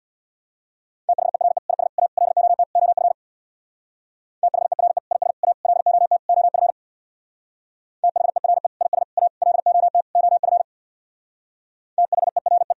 Morse code, heard as a keyed tone, about 65 wpm, 700 Hz. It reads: THERE IS A LOT OF THERE IS A LOT OF THERE IS A LOT OF THERE